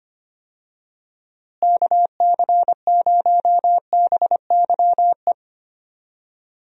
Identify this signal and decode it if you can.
Morse code, heard as a keyed tone, about 25 wpm, 700 Hz. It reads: KC0BYE
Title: USA Call Signs - 6 Characters - Repeated 2x - 25wpm